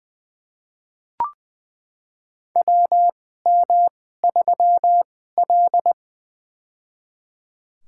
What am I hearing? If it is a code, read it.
Morse code, heard as a keyed tone, about 20 wpm, spaced as for 15 wpm, 700 Hz. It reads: WM3L
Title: USA Call Signs - 4 Characters - 2x Character Spacing - 20wpm